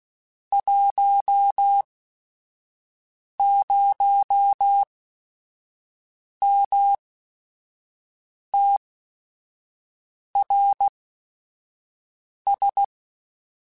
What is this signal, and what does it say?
Morse code, heard as a keyed tone, about 16 wpm, spaced as for 5 wpm, 800 Hz. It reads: 10MTRS
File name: QSOs / QSOs-mp3s / WD0FBT 1@5